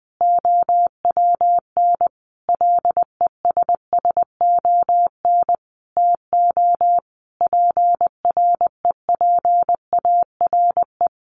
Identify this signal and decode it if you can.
Morse code, heard as a keyed tone, about 20 wpm, 700 Hz. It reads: OWN LESSON TO PREPARE
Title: Sentences from Top 1200 Words 20wpm